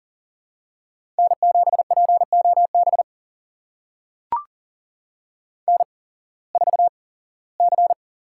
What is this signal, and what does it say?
Morse code, heard as a keyed tone, about 40 wpm, spaced as for 21 wpm, 700 Hz. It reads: N7POB N4C
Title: USA Call Signs - 8x Character Spacing - 40wpm